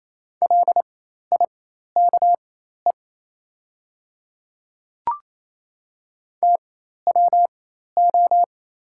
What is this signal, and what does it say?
Morse code, heard as a keyed tone, about 27 wpm, spaced as for 13 wpm, 690 Hz. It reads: LIKE TWO
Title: 4x Character Spacing - Top 100 Words - 28wpm